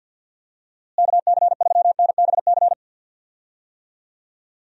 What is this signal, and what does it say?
Morse code, heard as a keyed tone, about 50 wpm, 700 Hz. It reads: KC3NBC
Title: USA Call Signs - 6 Characters - Repeated 2x - 50wpm